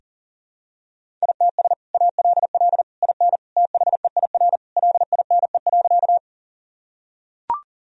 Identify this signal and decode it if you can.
Morse code, heard as a keyed tone, about 40 wpm, 700 Hz. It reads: ITS ALL IN THEIR LINE.
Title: Sentences from Top 1000 Words - Review Part 1 - 40wpm